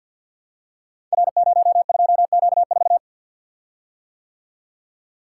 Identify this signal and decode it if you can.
Morse code, heard as a keyed tone, about 46 wpm, 700 Hz. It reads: R02Q4
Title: Code Groups - 5 Letter:Numbers - 50wpm